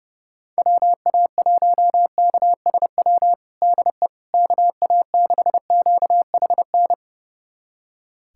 Morse code, heard as a keyed tone, about 32 wpm, 700 Hz. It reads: WA1KSW DE KA6QHN
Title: USA Callsign DE Callsign - 6 Characters - 30wpm